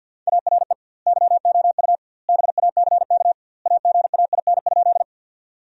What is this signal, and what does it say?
Morse code, heard as a keyed tone, about 50 wpm, 700 Hz. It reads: ARE YOU BACK AGAIN?
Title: 2x Word-Spacing - Sentences from Top 200 Words - 50wpm